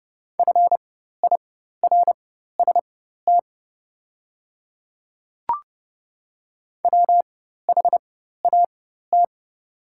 Morse code, heard as a keyed tone, about 30 wpm, spaced as for 14 wpm, 700 Hz. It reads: FIRST WHAT
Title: 4x Character Spacing - Top 100 Words - 30wpm